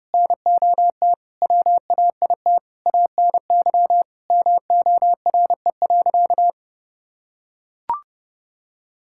Morse code, heard as a keyed tone, about 29 wpm, 700 Hz. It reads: NOT WAIT ANY MORE.
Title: Sentences from Top 1000 Words - Review Part 5 - 30wpm